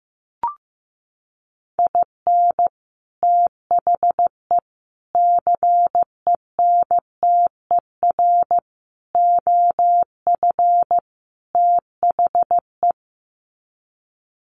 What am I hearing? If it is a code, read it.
Morse code, heard as a keyed tone, about 15 wpm, 700 Hz. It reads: IN THE CENTER OF THE
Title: Sets of 5 Words 15wpm